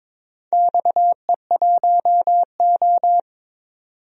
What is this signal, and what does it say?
Morse code, heard as a keyed tone, about 22 wpm, 700 Hz. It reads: XE1O
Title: DXCC-100 Call Signs - 4 Characters - 22wpm